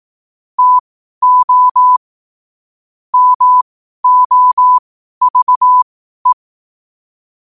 Morse code, heard as a keyed tone, about 18 wpm, spaced as for 12 wpm, 1000 Hz. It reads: TO MOVE
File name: qotdfiles / 12wpm